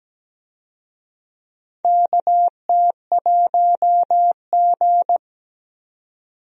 Morse code, heard as a keyed tone, about 17 wpm, 700 Hz. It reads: KT1G
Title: USA Call Signs - 4 Characters - 17wpm